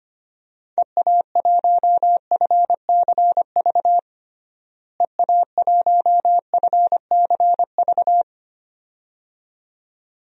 Morse code, heard as a keyed tone, about 25 wpm, 700 Hz. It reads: EA1FCV EA1FCV
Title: DXCC-100 Call Signs - 6 Characters - Repeated 2x - 25wpm